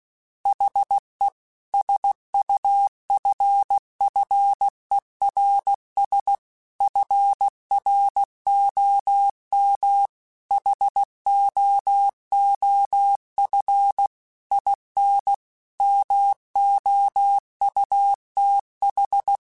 Morse code, heard as a keyed tone, about 16 wpm, 800 Hz. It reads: HE SUFFERS FROM HOOF IN MOUTH
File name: Quotes / Quotes-mp3s / HoofInMouth @15